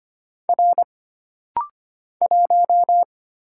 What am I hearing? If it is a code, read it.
Morse code, heard as a keyed tone, about 25 wpm, 700 Hz. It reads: R 1
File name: Single Character (Letter:Number:Punctuation:Prosign) - Rapid-Fire - 25wpm